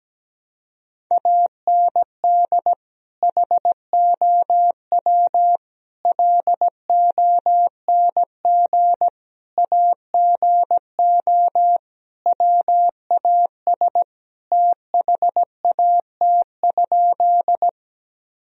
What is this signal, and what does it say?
Morse code, heard as a keyed tone, about 17 wpm, 700 Hz. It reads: AND HOW LONG AGO WAS THAT?
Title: Sentences from Top 1000 Words - Review Part 3 - 17wpm